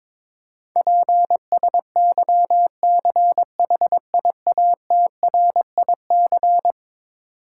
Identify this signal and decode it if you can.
Morse code, heard as a keyed tone, about 22 wpm, 700 Hz. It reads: PSYCHIATRIC
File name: practice / 22wpm / Eleven Letter Words - 22wpm